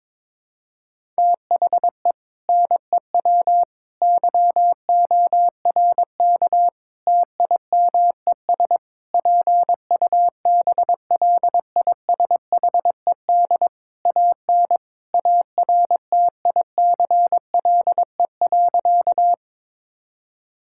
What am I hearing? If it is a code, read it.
Morse code, heard as a keyed tone, about 22 wpm, 700 Hz. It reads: THE NEW YORK TIMES PUBLISHED AN ARTICLE.
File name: practice / 22wpm / Sentences with 4-Word Phrases - 22wpm